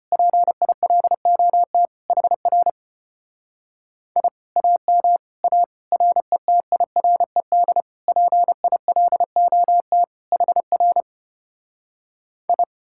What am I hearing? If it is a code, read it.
Morse code, heard as a keyed tone, about 33 wpm, 700 Hz. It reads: PILOT HR I AM A RETIRED PILOT HR I